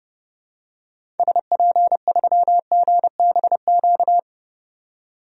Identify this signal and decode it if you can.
Morse code, heard as a keyed tone, about 30 wpm, 700 Hz. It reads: SP3GBQ
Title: DXCC-100 Call Signs - 6 Characters - 30wpm